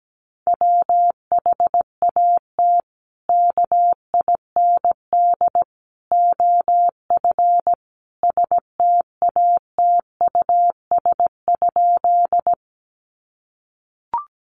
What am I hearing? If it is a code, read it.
Morse code, heard as a keyed tone, about 17 wpm, 700 Hz. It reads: WHAT KIND OF STATUS?
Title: Sentences from Top 1400 Words 17wpm